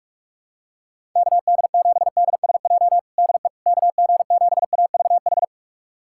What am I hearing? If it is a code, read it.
Morse code, heard as a keyed tone, about 46 wpm, 700 Hz. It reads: KD7DSJ DE KG7AVH
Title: USA Callsign DE Callsign 45wpm